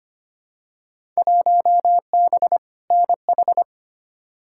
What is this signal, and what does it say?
Morse code, heard as a keyed tone, about 25 wpm, 700 Hz. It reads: 1B NH
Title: Field Day Exchanges Encore - 25wpm